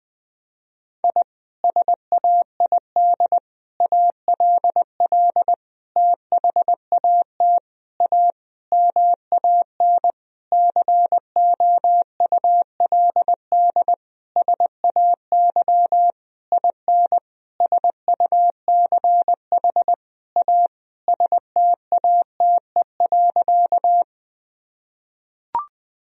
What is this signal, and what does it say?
Morse code, heard as a keyed tone, about 20 wpm, 700 Hz. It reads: I SAID ALL THAT A MAN COULD SAY IN SUCH A STATE.
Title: Sentences from Top 1000 Words - Review Part 2 - 20wpm